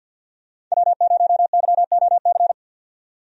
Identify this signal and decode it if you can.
Morse code, heard as a keyed tone, about 49 wpm, 700 Hz. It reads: W0YOC
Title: USA Call Signs - 5 Characters - 50wpm